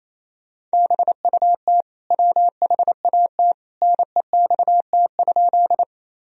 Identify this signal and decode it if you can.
Morse code, heard as a keyed tone, about 28 wpm, 700 Hz. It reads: BUT WHAT NEXT?
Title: Sentences from Top 1000 Words - Review Part 1 - 28wpm